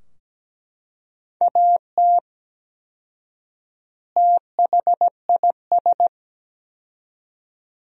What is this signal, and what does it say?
Morse code, heard as a keyed tone, about 17 wpm, 700 Hz. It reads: AT THIS
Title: 4x Word-Spacing - Sentences from Top 300 Words - 17wpm